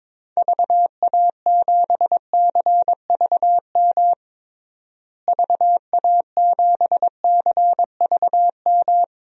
Morse code, heard as a keyed tone, about 22 wpm, 700 Hz. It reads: VA7CVM VA7CVM
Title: DXCC-100 Call Signs - 6 Characters - Repeated 2x - 22wpm